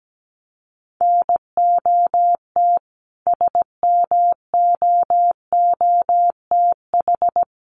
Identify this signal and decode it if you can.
Morse code, heard as a keyed tone, about 17 wpm, 700 Hz. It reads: NOT SMOOTH
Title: Sentences from Top 1400 Words 17wpm